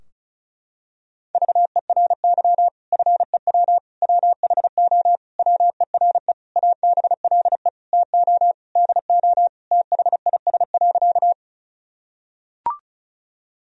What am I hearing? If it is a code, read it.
Morse code, heard as a keyed tone, about 36 wpm, 700 Hz. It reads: VERY FEW WHO WERE ABLE TO DO THIS.
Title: Sentences from Top 1000 Words - Review Part 2 - 35wpm